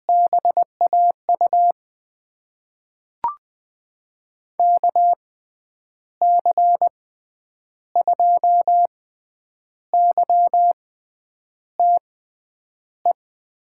Morse code, heard as a keyed tone, about 20 wpm, spaced as for 8 wpm, 700 Hz. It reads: BAU KC2YTE
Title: USA Call Signs - 6 Characters - 6x Character Spacing - 20wpm